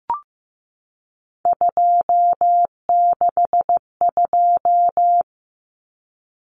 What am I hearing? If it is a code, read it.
Morse code, heard as a keyed tone, about 15 wpm, 700 Hz. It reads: 262
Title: Numbers - 3-Digit 15wpm